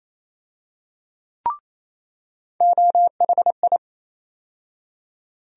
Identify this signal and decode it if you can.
Morse code, heard as a keyed tone, about 28 wpm, 700 Hz. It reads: OHI
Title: USA Call Sign Postfixes - 3 Letters 28wpm